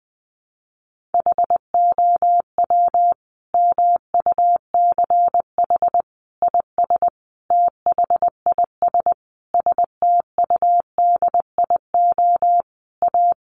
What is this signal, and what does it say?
Morse code, heard as a keyed tone, about 20 wpm, 700 Hz. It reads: HOW MUCH IS THIS STUDIO A
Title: Sentences from Top 1200 Words 20wpm